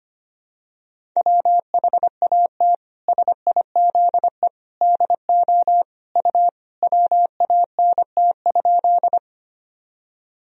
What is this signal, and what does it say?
Morse code, heard as a keyed tone, about 25 wpm, 700 Hz. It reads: WHAT SIZE DO U WANT?
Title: Sentences from Top 400 Words and Abbreviations 25wpm